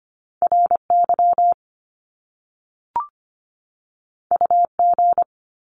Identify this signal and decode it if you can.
Morse code, heard as a keyed tone, about 25 wpm, 700 Hz. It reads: RY UG
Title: Top Two Letter Combinations in Words 25wpm